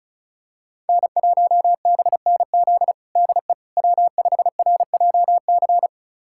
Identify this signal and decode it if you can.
Morse code, heard as a keyed tone, about 36 wpm, 700 Hz. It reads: N1BNZ DE W5RJC
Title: USA Callsign DE Callsign - 5 Characters - 35wpm